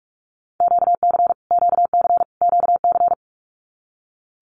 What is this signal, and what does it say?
Morse code, heard as a keyed tone, about 44 wpm, 700 Hz. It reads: QC QC QC
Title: Canadian Provinces and Territories - Repeated 3x - 45wpm